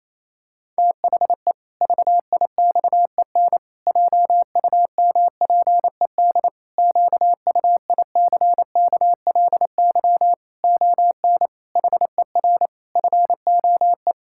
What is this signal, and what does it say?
Morse code, heard as a keyed tone, about 28 wpm, 700 Hz. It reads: THE VIXEN JUMPED QUICKLY ON HER FOE
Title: Pangram Challenge v2 - 28wpm